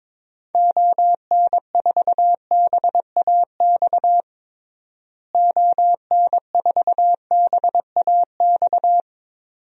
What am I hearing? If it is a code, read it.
Morse code, heard as a keyed tone, about 22 wpm, 700 Hz. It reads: ON4BAX ON4BAX
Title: DXCC-100 Call Signs - 6 Characters - Repeated 2x - 22wpm